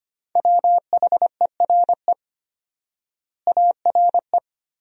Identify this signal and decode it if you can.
Morse code, heard as a keyed tone, about 25 wpm, 700 Hz. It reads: WHERE ARE
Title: 4x Word-Spacing - Sentences from Top 200 Words - 25wpm